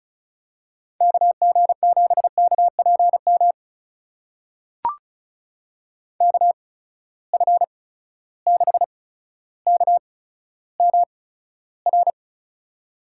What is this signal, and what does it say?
Morse code, heard as a keyed tone, about 36 wpm, spaced as for 15 wpm, 700 Hz. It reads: KG7KPM KF6KMR
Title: USA Call Signs - 6 Characters - 8x Character Spacing - 35wpm